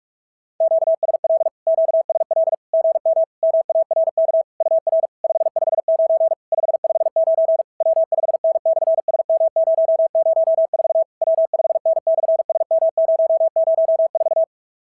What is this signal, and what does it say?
Morse code, heard as a keyed tone, about 42 wpm, 690 Hz. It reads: QSL QSL GM MARK UR 559 559 W5N/SM004 W5N/SM004
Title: SOTA QSO Practice 45wpm